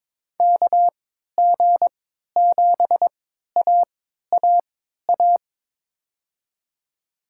Morse code, heard as a keyed tone, about 23 wpm, spaced as for 13 wpm, 700 Hz. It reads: KG7AAA
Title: USA Call Signs - 3x Character Spacing - 22wpm